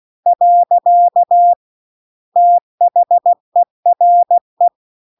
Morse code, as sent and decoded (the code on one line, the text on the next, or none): .-.-.- / - .... .-. .
. THERE